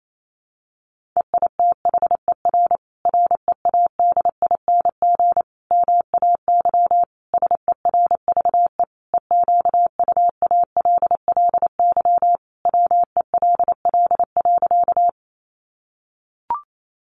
Morse code, heard as a keyed tone, about 28 wpm, 700 Hz. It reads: EITHER READING MAY SERVE EQUALLY WELL.
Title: Sentences from Top 1400 Words 28wpm